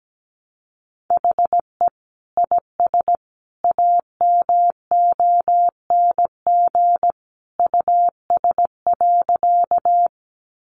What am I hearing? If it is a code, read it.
Morse code, heard as a keyed tone, about 17 wpm, 700 Hz. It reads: HE IS AMONG US.
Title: Sentences from Top 500 Words 17wpm